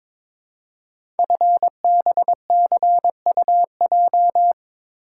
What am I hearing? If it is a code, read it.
Morse code, heard as a keyed tone, about 22 wpm, 700 Hz. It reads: FBCUJ
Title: Code Groups - 5 Letters - 22wpm